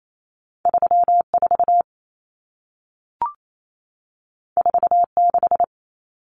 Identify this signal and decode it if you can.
Morse code, heard as a keyed tone, about 28 wpm, 700 Hz. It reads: 34 46
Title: Common Cut Numbers - 2-Digit 28wpm